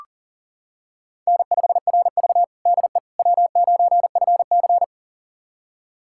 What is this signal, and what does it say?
Morse code, heard as a keyed tone, about 40 wpm, 700 Hz. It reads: N5RV DE W9FC
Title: USA Callsign DE Callsign - 4 Characters - 40wpm